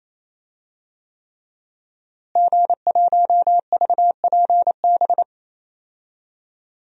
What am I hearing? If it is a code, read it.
Morse code, heard as a keyed tone, about 28 wpm, 700 Hz. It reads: G1VPB
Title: DXCC-100 Call Signs - 5 Characters - Repeated 3x - 28wpm